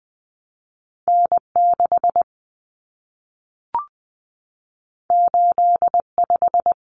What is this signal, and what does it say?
Morse code, heard as a keyed tone, about 20 wpm, 700 Hz. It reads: N6 85
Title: Common Cut Numbers - 2-Digit 20wpm